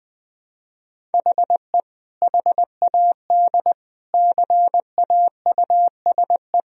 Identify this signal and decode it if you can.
Morse code, heard as a keyed tone, about 20 wpm, 700 Hz. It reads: HE HAD CAUSE
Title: Sentences from Top 1000 Words - Review Part 1 - 20wpm